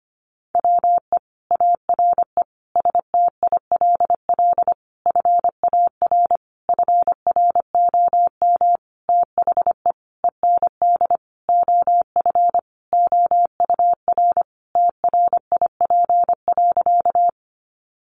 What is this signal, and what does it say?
Morse code, heard as a keyed tone, about 25 wpm, 700 Hz. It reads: WE ARE STILL FAR FROM THE END OF OUR TRIP.
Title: Sentences from Top 1300 Words 25wpm